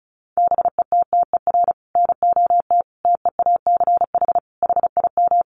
Morse code, heard as a keyed tone, about 34 wpm, 700 Hz. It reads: BETTER NOT TEACH HIM
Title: Sentences from Top 600 Words 35wpm